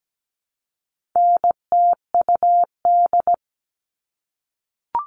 Morse code, heard as a keyed tone, about 17 wpm, 700 Hz. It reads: NTUD
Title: All Cut Numbers - 4-Digit 17wpm